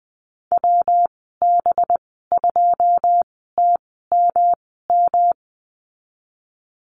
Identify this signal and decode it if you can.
Morse code, heard as a keyed tone, about 20 wpm, spaced as for 15 wpm, 700 Hz. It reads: WB2TMM
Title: 200 Lesson 32g - Letter k - Call Signs 20wpm